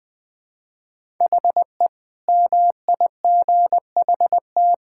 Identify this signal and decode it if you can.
Morse code, heard as a keyed tone, about 21 wpm, 700 Hz. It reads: HE MIGHT
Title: Sentences with 8-Word Phrases - 20wpm